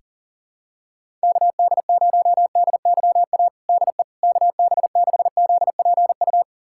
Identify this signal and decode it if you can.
Morse code, heard as a keyed tone, about 37 wpm, 700 Hz. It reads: KD0DYA DE KB6ZPU
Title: USA Callsign DE Callsign 40wpm